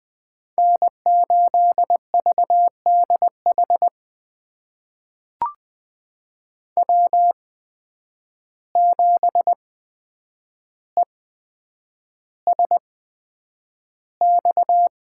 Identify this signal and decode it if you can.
Morse code, heard as a keyed tone, about 20 wpm, spaced as for 9 wpm, 700 Hz. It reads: N8VDH W7ESX
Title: USA Call Signs - 5 Characters - 8x Character Spacing - 20wpm